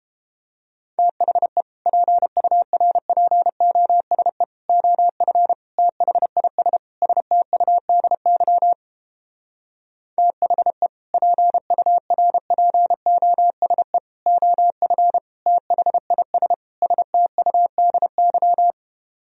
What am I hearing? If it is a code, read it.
Morse code, heard as a keyed tone, about 29 wpm, 700 Hz. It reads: THE PURPOSE OF THIS STUDY THE PURPOSE OF THIS STUDY